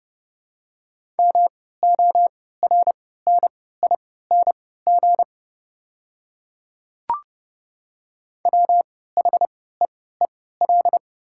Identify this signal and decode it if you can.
Morse code, heard as a keyed tone, about 30 wpm, spaced as for 17 wpm, 700 Hz. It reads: MORNING WHEEL
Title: 3x Character Spacing - Top 500 Words - 30wpm